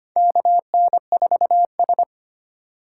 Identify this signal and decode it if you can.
Morse code, heard as a keyed tone, about 25 wpm, 700 Hz. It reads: KN4S
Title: USA Call Signs - 4 Characters - 2x Character Spacing - 25wpm